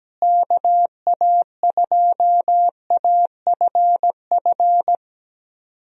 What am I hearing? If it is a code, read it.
Morse code, heard as a keyed tone, about 17 wpm, 700 Hz. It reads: KA2AFF
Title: USA Call Signs - 6 Characters - Repeated 3x - 17wpm